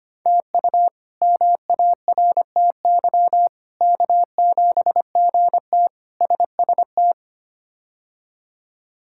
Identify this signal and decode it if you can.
Morse code, heard as a keyed tone, about 25 wpm, 700 Hz. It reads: TU MARTY K7GT SST
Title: K1USN SST QSO Practice - 25wpm